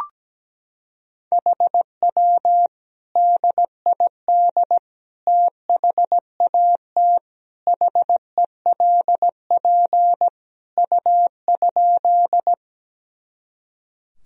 Morse code, heard as a keyed tone, about 17 wpm, 700 Hz. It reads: HW DID THAT HELP U?
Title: Sentences from Top 200 Words and Abbreviations 17wpm